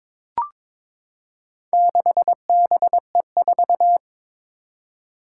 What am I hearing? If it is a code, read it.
Morse code, heard as a keyed tone, about 22 wpm, 700 Hz. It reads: 6BE4
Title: Four Challenging Letter-Number Combinations 22wpm